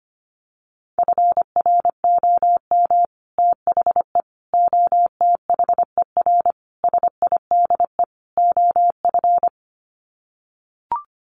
Morse code, heard as a keyed tone, about 25 wpm, 700 Hz. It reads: FROM THE OTHER SIDE OF
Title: Sets of 5 Words Encore 25wpm